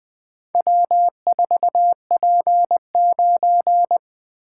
Morse code, heard as a keyed tone, about 20 wpm, 700 Hz. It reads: W4P9
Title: Four Letter-Number Combinations - 20wpm